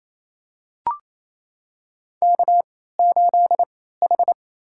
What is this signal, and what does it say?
Morse code, heard as a keyed tone, about 28 wpm, spaced as for 16 wpm, 700 Hz. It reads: K8H
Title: USA Call Signs - 3x Character Spacing - 28wpm